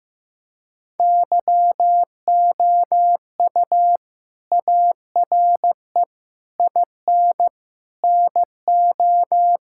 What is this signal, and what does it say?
Morse code, heard as a keyed tone, about 15 wpm, 700 Hz. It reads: YOU ARE IN NO
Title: Sentences from Top 1600 Words 15wpm